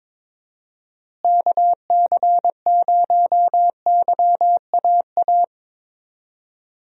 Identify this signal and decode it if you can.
Morse code, heard as a keyed tone, about 22 wpm, 700 Hz. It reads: KC0YAA
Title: USA Call Signs - 6 Characters - Repeated 3x - 22wpm